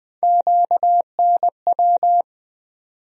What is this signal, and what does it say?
Morse code, heard as a keyed tone, about 20 wpm, 700 Hz. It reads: QNW
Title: USA Call Sign Postfixes - 3 Letters 20wpm